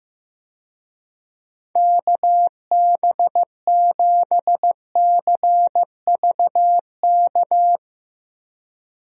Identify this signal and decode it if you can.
Morse code, heard as a keyed tone, about 15 wpm, 700 Hz. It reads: KB7CVK
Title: USA Call Signs - 6 Characters - Repeated 2x - 15wpm